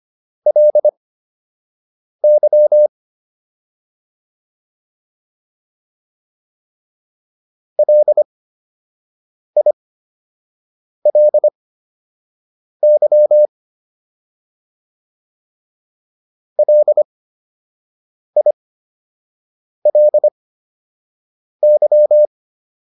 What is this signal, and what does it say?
Morse code, heard as a keyed tone, about 25 wpm, spaced as for 6 wpm, 600 Hz. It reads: LY LILY LILY